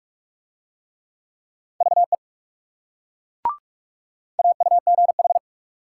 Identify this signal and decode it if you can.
Morse code, heard as a keyed tone, about 37 wpm, 700 Hz. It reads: VE AUGH
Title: Speed-Racing Multi-letter Phonemes - 30wpm